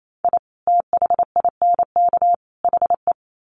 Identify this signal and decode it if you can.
Morse code, heard as a keyed tone, about 28 wpm, 700 Hz. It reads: I THINK HE
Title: Sets of 3 Words 28wpm